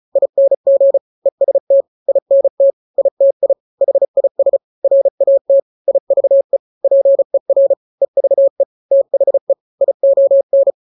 Morse code, heard as a keyed tone, about 34 wpm, 550 Hz. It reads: ING EST INT ITI HIS RAT IVE PER EVE THE ION
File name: 3-grams 35 30 WPM